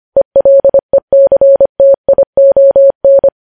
Morse code, heard as a keyed tone, about 25 wpm, 550 Hz. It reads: ELECTION
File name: podcast / News headlines-1x-25x25